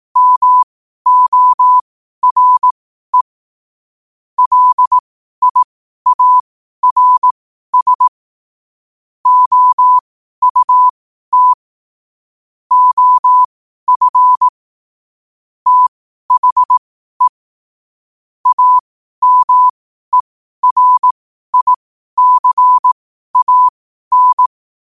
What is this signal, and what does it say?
Morse code, heard as a keyed tone, about 18 wpm, spaced as for 12 wpm, 1000 Hz. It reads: MORE LIARS OUT OF THE AMERICAN
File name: qotdfiles / 12wpm